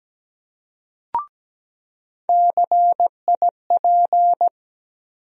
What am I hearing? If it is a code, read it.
Morse code, heard as a keyed tone, about 17 wpm, 700 Hz. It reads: CIP
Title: USA Call Sign Postfixes - 3 Letters 17wpm